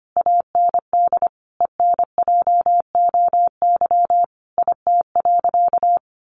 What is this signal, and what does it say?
Morse code, heard as a keyed tone, about 25 wpm, 700 Hz. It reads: AND ENJOY IT.
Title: Sentences from Top 1400 Words 25wpm